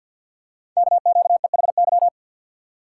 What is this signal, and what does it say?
Morse code, heard as a keyed tone, about 47 wpm, 700 Hz. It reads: KQEHY
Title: Code Groups - 5 Letters - 50wpm